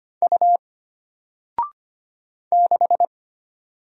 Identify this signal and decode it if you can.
Morse code, heard as a keyed tone, about 25 wpm, 700 Hz. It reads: U 6
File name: Single-Letter-Number-25wpm